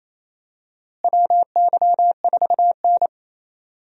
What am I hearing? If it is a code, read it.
Morse code, heard as a keyed tone, about 27 wpm, 700 Hz. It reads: WY4N
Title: USA Call Signs - 4 Characters - Repeated 2x - 28wpm